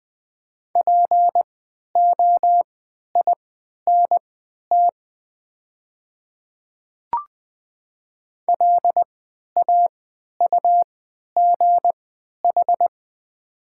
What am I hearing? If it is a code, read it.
Morse code, heard as a keyed tone, about 20 wpm, spaced as for 11 wpm, 700 Hz. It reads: POINT LAUGH
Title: 3x Character Spacing - Top 500 Words - 20wpm